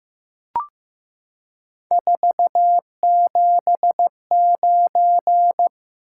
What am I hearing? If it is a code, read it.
Morse code, heard as a keyed tone, about 15 wpm, 700 Hz. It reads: 479
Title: Top 180 QSO Elements 15wpm